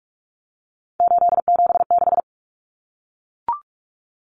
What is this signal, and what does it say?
Morse code, heard as a keyed tone, about 44 wpm, 700 Hz. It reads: 876
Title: Numbers - 3-Digit 45wpm